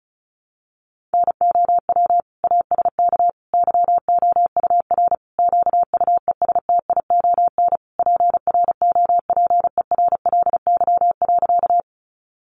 Morse code, heard as a keyed tone, about 34 wpm, 700 Hz. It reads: NOW ASK YOUR QUESTION PROPERLY.
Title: Sentences from Top 1100 Words 35wpm